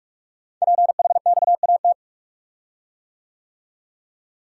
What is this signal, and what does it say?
Morse code, heard as a keyed tone, about 44 wpm, 700 Hz. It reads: PHXAT
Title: Code Groups - 5 Letter:Numbers - 45wpm